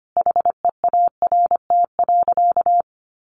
Morse code, heard as a keyed tone, about 25 wpm, 700 Hz. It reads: HEART.
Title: Sentences from Top 1200 Words 25wpm